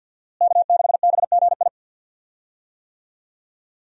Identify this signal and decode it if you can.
Morse code, heard as a keyed tone, about 52 wpm, 700 Hz. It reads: K6BGI
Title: USA Call Signs - 5 Characters - Repeated 2x - 50wpm